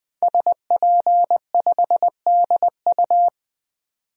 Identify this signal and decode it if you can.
Morse code, heard as a keyed tone, about 20 wpm, 700 Hz. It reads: SP5DU
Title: DXCC-100 Call Signs - 5 Characters - 20wpm